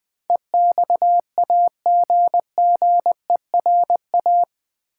Morse code, heard as a keyed tone, about 20 wpm, 700 Hz. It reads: EXAGGERA
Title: Build A Word - 4 Letters At A Time - 10-Letter Words - 20wpm